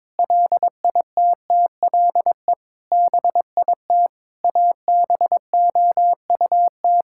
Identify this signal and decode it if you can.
Morse code, heard as a keyed tone, about 22 wpm, 700 Hz. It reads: LITTLE BIT ABOUT
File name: practice / 22wpm / Sentences with 8-Word Phrases - 22wpm